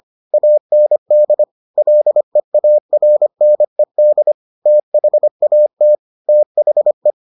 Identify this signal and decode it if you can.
Morse code, heard as a keyed tone, about 25 wpm, 600 Hz. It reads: AND LEARNED THAT THE